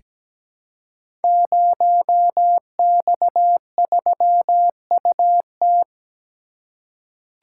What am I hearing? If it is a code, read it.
Morse code, heard as a keyed tone, about 17 wpm, 700 Hz. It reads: 0X3UT
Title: Code Groups - 5 Letter:Numbers - 17wpm